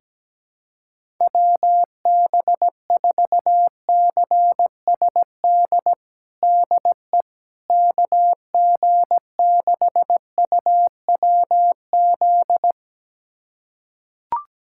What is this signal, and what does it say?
Morse code, heard as a keyed tone, about 17 wpm, 700 Hz. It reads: WB4CSD DE KG6UWZ
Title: USA Callsign DE Callsign 17wpm